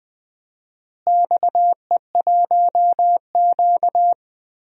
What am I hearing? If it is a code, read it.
Morse code, heard as a keyed tone, about 20 wpm, 700 Hz. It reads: XE1Q
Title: DXCC-100 Call Signs - 4 Characters - Repeated 2x - 20wpm